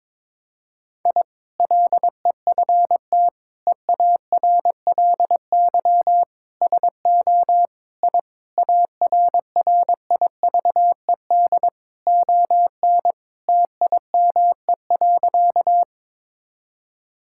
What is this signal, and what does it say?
Morse code, heard as a keyed tone, about 22 wpm, 700 Hz. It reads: I LEFT EARLY SO I ARRIVED ON TIME.
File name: practice / 22wpm / Sentences with 2-Word Phrases - 22wpm